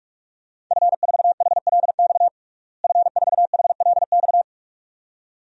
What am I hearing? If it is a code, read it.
Morse code, heard as a keyed tone, about 46 wpm, 700 Hz. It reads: F4HLX F4HLX
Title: DXCC-100 Call Signs - 5 Characters - Repeated 2x - 45wpm